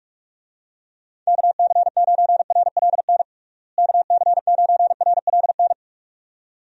Morse code, heard as a keyed tone, about 46 wpm, 700 Hz. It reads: KC9RLN KC9RLN
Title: USA Call Signs - 6 Characters - Repeated 2x - 45wpm